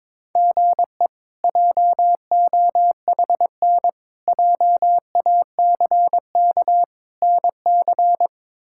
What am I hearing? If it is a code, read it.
Morse code, heard as a keyed tone, about 22 wpm, 700 Hz. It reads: GE JOHN JACK NC
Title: K1USN SST QSO Practice - 22wpm